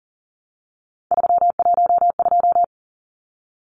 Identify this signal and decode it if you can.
Morse code, heard as a keyed tone, about 40 wpm, 700 Hz. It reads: 312
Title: Numbers - 3-Digit 40wpm